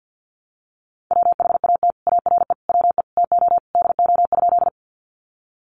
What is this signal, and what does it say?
Morse code, heard as a keyed tone, about 50 wpm, 700 Hz. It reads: WHAT ARE WE TO DO?
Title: Sentences from Top 100 Words 50wpm